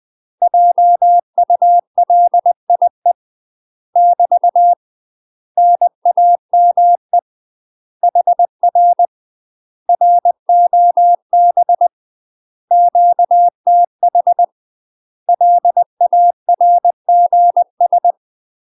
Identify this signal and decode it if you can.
Morse code, heard as a keyed tone, about 20 wpm, 700 Hz. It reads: JULIE <BT> NAME HR ROB QTH LARGS
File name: UK MASTER QSOs TRAINING/20WPM